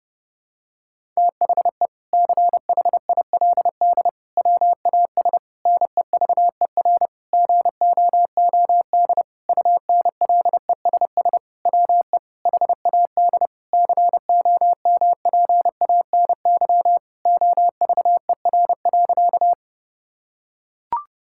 Morse code, heard as a keyed tone, about 32 wpm, 700 Hz. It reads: THE CHILD WAS NEVER GOOD UNLESS WE HAD COMPANY OVER.
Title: Sentences from Top 1000 Words - Review Part 6 - 30wpm